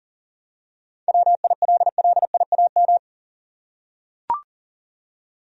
Morse code, heard as a keyed tone, about 40 wpm, 700 Hz. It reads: WILLIAM
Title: USA Common Names - 40wpm